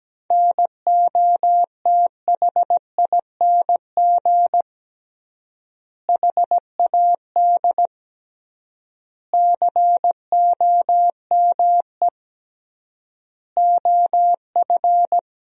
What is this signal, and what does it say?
Morse code, heard as a keyed tone, about 17 wpm, 700 Hz. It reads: NOTHING HAD COME OF
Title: 3x Word-Spacing - Sentences from Top 500 Words - 17wpm